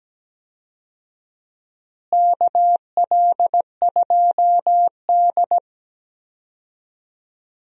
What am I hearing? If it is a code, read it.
Morse code, heard as a keyed tone, about 17 wpm, 700 Hz. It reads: KL2D
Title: USA Call Signs - 4 Characters - Repeated 2x - 17wpm